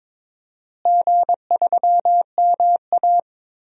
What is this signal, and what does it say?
Morse code, heard as a keyed tone, about 22 wpm, 700 Hz. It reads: G3MA
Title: DXCC-100 Call Signs - 4 Characters - Repeated 3x - 22wpm